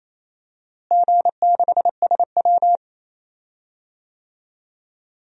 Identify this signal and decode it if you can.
Morse code, heard as a keyed tone, about 28 wpm, 700 Hz. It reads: G6SW
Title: DXCC-100 Call Signs - 4 Characters - Repeated 2x - 28wpm